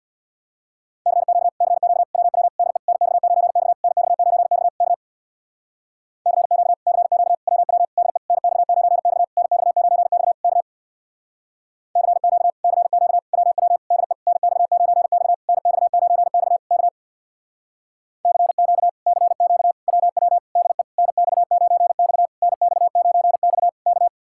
Morse code, heard as a keyed tone, about 64 wpm, 700 Hz. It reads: CQ CQ WW DE NX9X NX9X K CQ CQ WW DE NX9X NX9X K CQ CQ WW DE NX9X NX9X K CQ CQ WW DE NX9X NX9X K